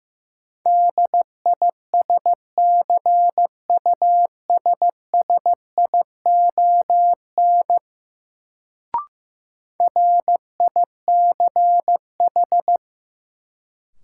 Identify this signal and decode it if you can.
Morse code, heard as a keyed tone, about 15 wpm, 700 Hz. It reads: DISCUSSION RICH
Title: Sentences from Top 1100 Words - New Word Drill - 15wpm